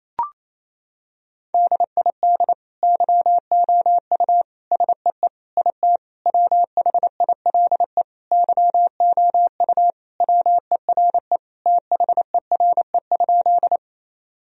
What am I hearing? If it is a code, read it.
Morse code, heard as a keyed tone, about 28 wpm, 700 Hz. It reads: DID YOU SEE IT WHILE YOU WERE THERE?
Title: Sentences from Top 1000 Words - Review Part 1 - 28wpm